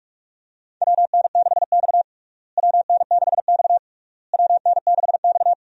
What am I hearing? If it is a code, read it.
Morse code, heard as a keyed tone, about 46 wpm, 700 Hz. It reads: WN6X WN6X WN6X
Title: USA Call Signs - 4 Characters - Repeated 3x - 45wpm